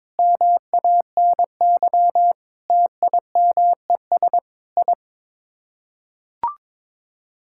Morse code, heard as a keyed tone, about 22 wpm, 700 Hz. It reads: MANY TIMES I
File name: practice / 22wpm / Sets of 8 Words - 22wpm